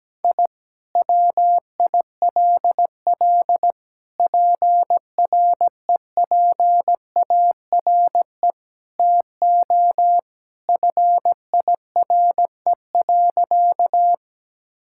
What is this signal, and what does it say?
Morse code, heard as a keyed tone, about 17 wpm, 700 Hz. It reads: I WILL PREPARE TO FIRE.
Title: Sentences from Top 1000 Words - Review Part 6 - 17wpm